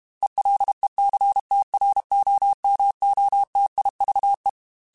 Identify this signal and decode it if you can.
Morse code, heard as a keyed tone, about 32 wpm, 800 Hz. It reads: ELECTROMOTIVE
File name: Technical Subjects / Technical Subjects-mp3s / XL @30